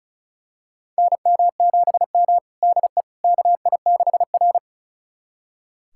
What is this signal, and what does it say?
Morse code, heard as a keyed tone, about 36 wpm, 700 Hz. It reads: NM7M DE KI6R
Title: USA Callsign DE Callsign - 4 Characters - 35wpm